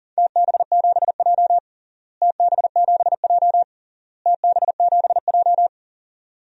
Morse code, heard as a keyed tone, about 37 wpm, 700 Hz. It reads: TB7J TB7J TB7J
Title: DXCC-100 Call Signs - 4 Characters - Repeated 3x - 40wpm